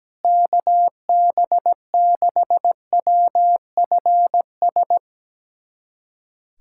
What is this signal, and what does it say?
Morse code, heard as a keyed tone, about 17 wpm, 700 Hz. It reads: KB6WFS
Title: USA Callsign DE Callsign - 6 Characters - 17wpm